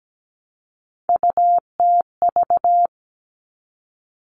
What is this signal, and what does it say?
Morse code, heard as a keyed tone, about 17 wpm, 700 Hz. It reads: UTV
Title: All Cut Numbers - 3-Digit 17wpm